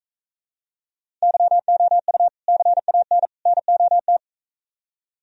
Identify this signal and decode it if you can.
Morse code, heard as a keyed tone, about 45 wpm, 700 Hz. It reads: YOU CAN NOT